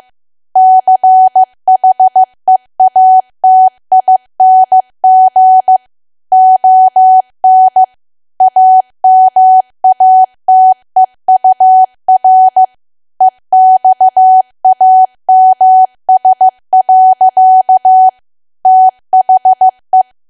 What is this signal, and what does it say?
Morse code, heard as a keyed tone, about 15 wpm, 750 Hz. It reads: CHEATING ON AMATEUR EXAMS. THE
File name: cwa / W1AW-Level2 / 15WPM